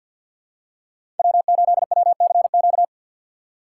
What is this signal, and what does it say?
Morse code, heard as a keyed tone, about 53 wpm, 700 Hz. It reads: W8WCQ
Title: USA Call Signs - 6x Character Spacing - 50wpm